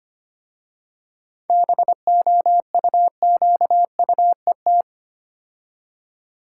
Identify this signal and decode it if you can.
Morse code, heard as a keyed tone, about 25 wpm, 700 Hz. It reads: BOUQUET
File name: practice / 25wpm / Difficult Words with Hints - 25wpm